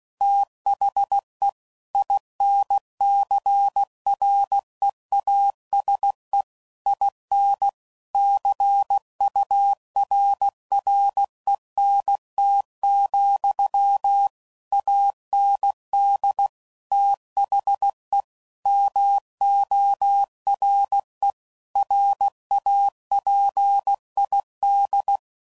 Morse code, heard as a keyed tone, about 16 wpm, 800 Hz. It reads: THE INCREASE IN CURRENT, AND THE MORE RAPID
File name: CW Code Groups @15WPM / Technical Subjects / XL